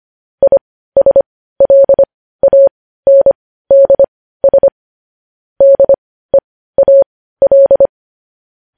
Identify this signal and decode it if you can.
Morse code, heard as a keyed tone, about 25 wpm, spaced as for 15 wpm, 550 Hz. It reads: ISLANDS DEAL
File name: podcast / News headlines-1x-25x15